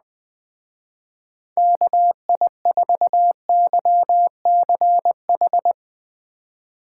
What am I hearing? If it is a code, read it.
Morse code, heard as a keyed tone, about 20 wpm, 700 Hz. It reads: KI4YCH
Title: Speed Racing - USA 2x3 Call Signs 17wpm